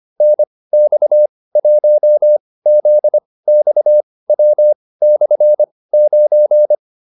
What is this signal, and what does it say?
Morse code, heard as a keyed tone, about 25 wpm, spaced as for 18 wpm, 600 Hz. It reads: NX1ZXW/9
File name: QSO307 18